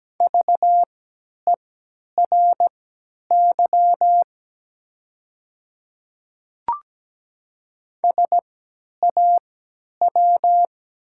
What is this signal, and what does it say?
Morse code, heard as a keyed tone, about 17 wpm, spaced as for 10 wpm, 700 Hz. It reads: VERY SAW
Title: 3x Character Spacing - Top 500 Words - 17wpm